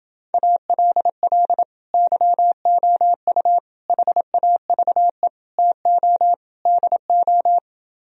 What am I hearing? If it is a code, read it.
Morse code, heard as a keyed tone, about 28 wpm, 700 Hz. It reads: ALL YOU HAVE TO DO